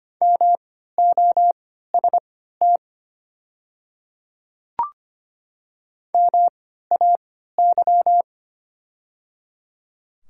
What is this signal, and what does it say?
Morse code, heard as a keyed tone, about 25 wpm, spaced as for 14 wpm, 700 Hz. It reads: MOST MAY
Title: 3x Character Spacing - Top 100 Words - 25wpm